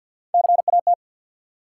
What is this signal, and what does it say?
Morse code, heard as a keyed tone, about 47 wpm, 700 Hz. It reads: CAT
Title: Three Letter Words - 50wpm